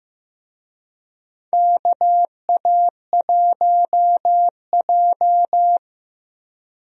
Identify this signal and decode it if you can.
Morse code, heard as a keyed tone, about 15 wpm, 700 Hz. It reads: KA1J
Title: USA Call Signs - 4 Characters - 15wpm